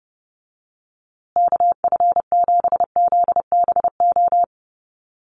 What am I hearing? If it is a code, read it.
Morse code, heard as a keyed tone, about 30 wpm, 700 Hz. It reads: KF7ZBO
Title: USA Call Signs Repeated 3x 30wpm